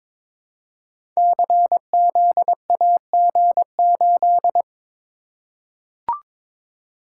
Morse code, heard as a keyed tone, about 22 wpm, 700 Hz. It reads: CZAG8
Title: Code Groups - 5 Letter:Numbers - 22wpm